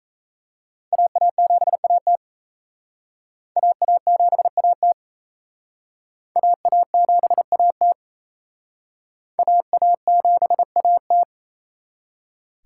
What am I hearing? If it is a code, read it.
Morse code, heard as a keyed tone, about 38 wpm, 700 Hz. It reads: AA7AT AA7AT AA7AT AA7AT